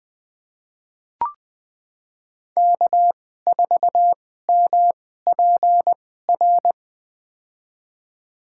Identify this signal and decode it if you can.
Morse code, heard as a keyed tone, about 20 wpm, spaced as for 14 wpm, 700 Hz. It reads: K4MPR
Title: USA Call Signs - 5 Characters - 2x Character Spacing - 20wpm